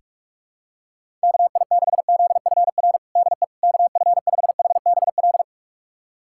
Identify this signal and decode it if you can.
Morse code, heard as a keyed tone, about 46 wpm, 700 Hz. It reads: KI6ZFR DE KF5HBL
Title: USA Callsign DE Callsign - 6 Characters - 45wpm